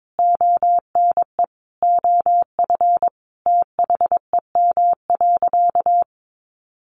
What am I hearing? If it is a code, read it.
Morse code, heard as a keyed tone, about 22 wpm, 700 Hz. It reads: ONE OF THEM.
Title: Additional Easy Phrases 22wpm